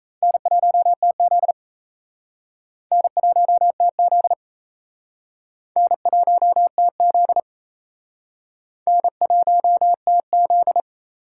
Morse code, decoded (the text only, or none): N1TZ N1TZ N1TZ N1TZ